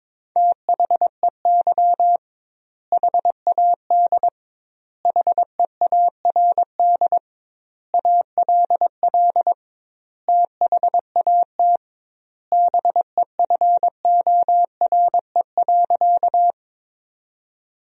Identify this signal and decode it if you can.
Morse code, heard as a keyed tone, about 22 wpm, 700 Hz. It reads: THEY HAD HEARD ALL THAT BEFORE.
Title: 2x Word-Spacing - Sentences from Top 400 Words - 22wpm